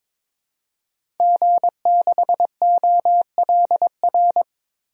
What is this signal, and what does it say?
Morse code, heard as a keyed tone, about 22 wpm, 700 Hz. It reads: G6OLR
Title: DXCC-100 Call Signs - 5 Characters - 22wpm